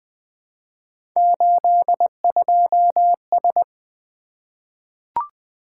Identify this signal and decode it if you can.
Morse code, heard as a keyed tone, about 20 wpm, 700 Hz. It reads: 82S
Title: Three Letter-Number Combinations - 20wpm